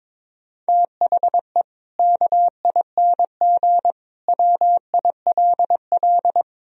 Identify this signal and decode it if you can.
Morse code, heard as a keyed tone, about 22 wpm, 700 Hz. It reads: THE KING WILL
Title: Sentences from Top 1000 Words - Review Part 3 - 22wpm